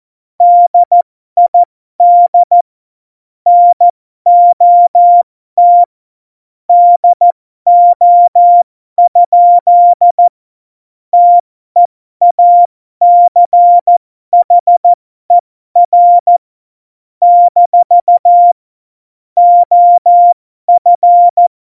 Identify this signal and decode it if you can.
Morse code, heard as a keyed tone, about 14 wpm, spaced as for 12 wpm, 700 Hz. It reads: DID NOT DO? TEACHER - OF
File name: Morsekurs / Witze